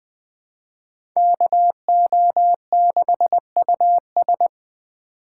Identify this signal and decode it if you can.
Morse code, heard as a keyed tone, about 21 wpm, 700 Hz. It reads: KO6US
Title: USA Call Signs - 5 Characters - 4x Character Spacing - 20wpm